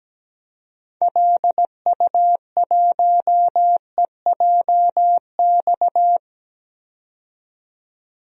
Morse code, heard as a keyed tone, about 17 wpm, 700 Hz. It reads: LU1EJX
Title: DXCC-100 Call Signs - 6 Characters - 17wpm